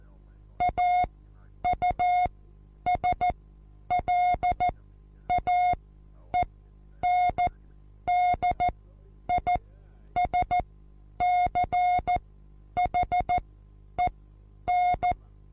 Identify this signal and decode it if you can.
Morse code, heard as a keyed tone, about 14 wpm, spaced as for 9 wpm, 720 Hz. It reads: AUSLAENDISCHEN
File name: Onlinelehrgang / morsen / DJ4UF Morsen23